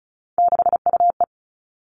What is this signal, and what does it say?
Morse code, heard as a keyed tone, about 36 wpm, 700 Hz. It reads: 6UE
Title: Three Challenging Letter-Number Combinations 35wpm